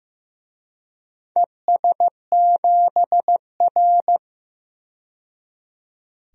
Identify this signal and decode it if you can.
Morse code, heard as a keyed tone, about 15 wpm, 700 Hz. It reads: ES7R
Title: DXCC-100 Call Signs - 4 Characters - 15wpm